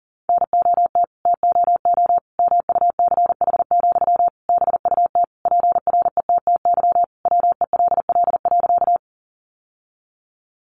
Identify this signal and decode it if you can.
Morse code, heard as a keyed tone, about 40 wpm, 700 Hz. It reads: NOT TOO MUCH, BUT PRETTY WELL.
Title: Sentences from Top 1000 Words 40wpm